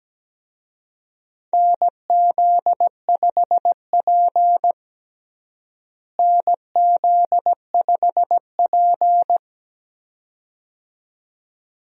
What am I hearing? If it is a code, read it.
Morse code, heard as a keyed tone, about 17 wpm, 700 Hz. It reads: NZ5P NZ5P
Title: USA Call Signs - 4 Characters - Repeated 2x - 17wpm